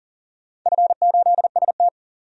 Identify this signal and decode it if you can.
Morse code, heard as a keyed tone, about 42 wpm, 700 Hz. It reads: F8ST
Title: DXCC-100 Call Signs - 4 Characters - 40wpm